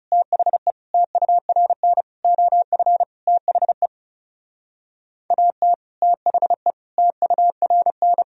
Morse code, heard as a keyed tone, about 34 wpm, 700 Hz. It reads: THE TURN OF THE AT THE TURN
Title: Speed-Racing - Sets of 5 Words 30wpm